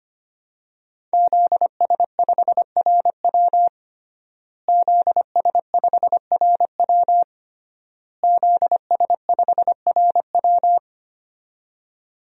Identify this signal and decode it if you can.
Morse code, heard as a keyed tone, about 25 wpm, 700 Hz. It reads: ZS5RW ZS5RW ZS5RW
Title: DXCC-100 Call Signs - 5 Characters - Repeated 3x - 25wpm